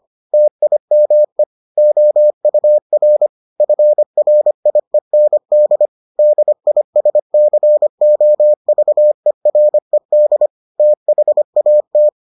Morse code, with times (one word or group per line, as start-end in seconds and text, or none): 0.3-1.4 TIME
1.8-3.3 OUR
3.6-5.9 FRIEND
6.2-10.5 DISCOVERED
10.8-12.1 THAT